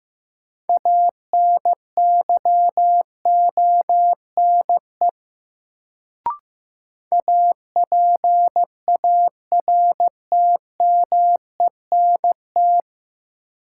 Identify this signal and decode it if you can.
Morse code, heard as a keyed tone, about 15 wpm, 700 Hz. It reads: ANYONE APARTMENT
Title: Sentences from Top 900 Words - New Word Drill - 15wpm